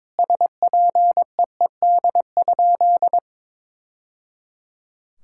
Morse code, heard as a keyed tone, about 22 wpm, 700 Hz. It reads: SPEED?
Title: Sentences from Top 1000 Words - Review Part 5 - 22wpm